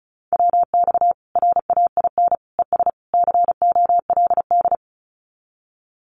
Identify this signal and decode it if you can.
Morse code, heard as a keyed tone, about 34 wpm, 700 Hz. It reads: WX RAIN ES COLD
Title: Common QSO Phrases 35wpm